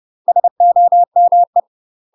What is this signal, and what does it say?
Morse code, heard as a keyed tone, about 30 wpm, 700 Hz. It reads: SOME
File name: UK MASTER QSOs TRAINING / 30WPM / UK MASTER QSOs 0045